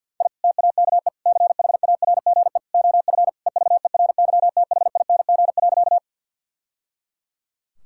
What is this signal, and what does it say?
Morse code, heard as a keyed tone, about 50 wpm, 700 Hz. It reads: I TAKE CHARGE OF EVERYTHING.
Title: Sentences from Top 1000 Words - Review Part 6 - 50wpm